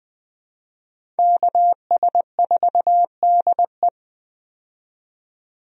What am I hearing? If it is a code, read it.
Morse code, heard as a keyed tone, about 20 wpm, 700 Hz. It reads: KS4DE
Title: USA Call Signs - 5 Characters - 20wpm